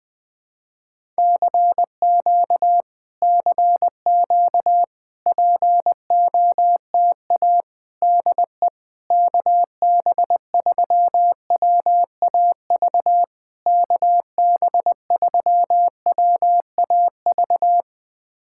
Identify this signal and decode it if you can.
Morse code, heard as a keyed tone, about 20 wpm, 690 Hz. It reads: CQ CQ POTA DE KB3WAV KB3WAV
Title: POTA QSO Practice - 20wpm